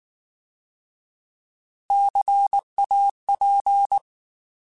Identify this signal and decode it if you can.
Morse code, heard as a keyed tone, about 19 wpm, 800 Hz. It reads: CAP
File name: Alphabetical / Alphabetical mp3s / Cs @18